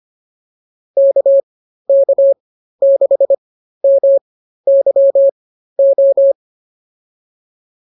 Morse code, heard as a keyed tone, about 25 wpm, spaced as for 13 wpm, 550 Hz. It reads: KK6MYO